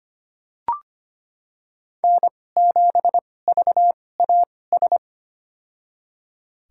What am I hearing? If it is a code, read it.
Morse code, heard as a keyed tone, about 25 wpm, spaced as for 18 wpm, 700 Hz. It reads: N7VAS
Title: USA Call Signs - 2x Character Spacing - 25wpm